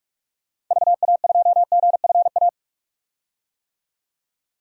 Morse code, heard as a keyed tone, about 45 wpm, 700 Hz. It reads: VA2GFA
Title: DXCC-100 Call Signs - 6 Characters - 45wpm